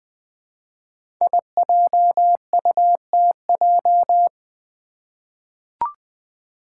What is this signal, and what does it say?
Morse code, heard as a keyed tone, about 20 wpm, 700 Hz. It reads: IJUTJ
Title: Five Challenging Letter-Number Combinations 20wpm